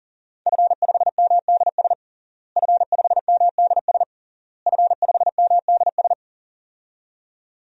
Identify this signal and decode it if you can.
Morse code, heard as a keyed tone, about 40 wpm, 700 Hz. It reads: F5MDS F5MDS F5MDS
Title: DXCC-100 Call Signs - 5 Characters - Repeated 3x - 40wpm